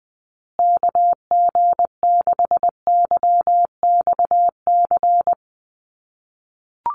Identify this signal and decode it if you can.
Morse code, heard as a keyed tone, about 20 wpm, 700 Hz. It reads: KG6YXC
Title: USA Call Signs Repeated 2x 20wpm